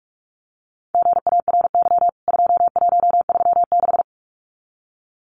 Morse code, heard as a keyed tone, about 44 wpm, 700 Hz. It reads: GARY 2136
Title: CWT Exchange 45wpm